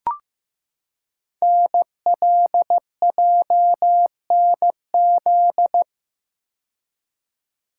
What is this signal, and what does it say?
Morse code, heard as a keyed tone, about 15 wpm, 700 Hz. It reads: NLJNZ
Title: Code Groups - 5 Letters - 15wpm